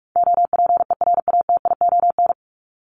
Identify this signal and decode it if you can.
Morse code, heard as a keyed tone, about 44 wpm, 700 Hz. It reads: OPERATION
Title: Top 1501-1750 Words 45wpm